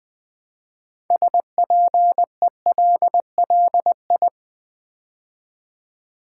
Morse code, heard as a keyed tone, about 20 wpm, 700 Hz. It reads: SPELLI
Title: Build A Word - 2 Letters At A Time - 8-Letter Words - 20wpm